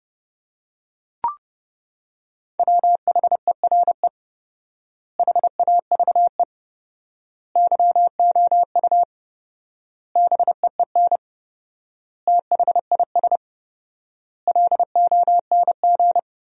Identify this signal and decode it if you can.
Morse code, heard as a keyed tone, about 30 wpm, 700 Hz. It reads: WHERE HAVE YOU BEEN THIS LONG
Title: 4x Word-Spacing - Sentences from Top 300 Words - 30wpm